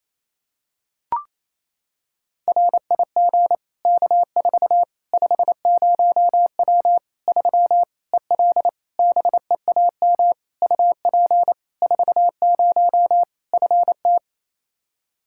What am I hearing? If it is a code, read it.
Morse code, heard as a keyed tone, about 28 wpm, 700 Hz. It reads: RIG K4 50W 3 EL BEAM UP 40 FT
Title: Practice QSO - Exchange 2 - Info - 28wpm